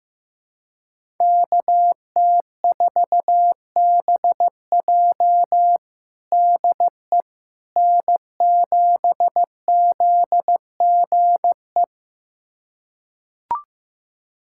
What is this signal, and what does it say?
Morse code, heard as a keyed tone, about 15 wpm, 700 Hz. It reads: KT4BJ DE N7ZGE
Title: USA Callsign DE Callsign - 5 Characters - 15wpm